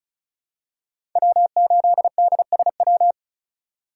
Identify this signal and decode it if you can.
Morse code, heard as a keyed tone, about 34 wpm, 700 Hz. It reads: W8DSW
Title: USA Call Signs - 2x Character Spacing - 35wpm